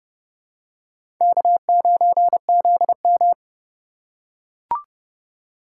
Morse code, heard as a keyed tone, about 31 wpm, 700 Hz. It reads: K9ZM
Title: USA Call Signs - 4 Characters - 4x Character Spacing - 30wpm